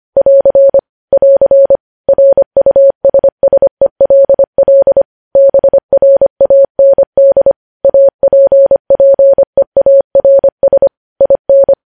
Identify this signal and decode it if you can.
Morse code, heard as a keyed tone, about 25 wpm, 550 Hz. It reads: <AR> <AR> RUSSELL BRAND APPEARS IN
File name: podcast / News headlines-1x-25x25